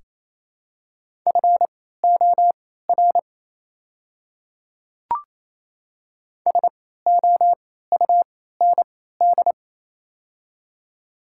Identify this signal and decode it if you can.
Morse code, heard as a keyed tone, about 27 wpm, spaced as for 16 wpm, 700 Hz. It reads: FOR SOUND
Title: 3x Character Spacing - Top 300 Words - 28wpm